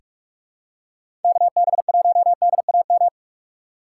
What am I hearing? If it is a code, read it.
Morse code, heard as a keyed tone, about 44 wpm, 700 Hz. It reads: KB1DAM
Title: USA Call Signs - 6 Characters - 2x Character Spacing - 45wpm